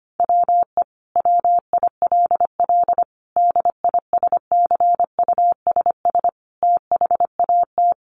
Morse code, heard as a keyed tone, about 25 wpm, 700 Hz. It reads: WE WILL DISCUSS THAT
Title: Sentences from Top 1000 Words 25wpm